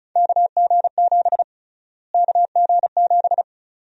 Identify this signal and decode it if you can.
Morse code, heard as a keyed tone, about 36 wpm, 700 Hz. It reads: KG7 KG7
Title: USA Call Sign Prefixes - Repeated 2x - 35wpm